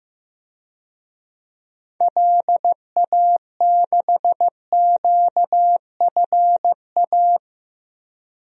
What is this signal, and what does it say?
Morse code, heard as a keyed tone, about 15 wpm, 700 Hz. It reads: LA6QFA
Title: DXCC-100 Call Signs - 6 Characters - Repeated 2x - 15wpm